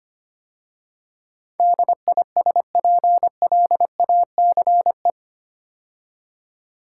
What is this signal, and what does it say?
Morse code, heard as a keyed tone, about 25 wpm, 700 Hz. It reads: DISPLACE
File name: 8 Letter Words - First 3 Letters Given - 25wpm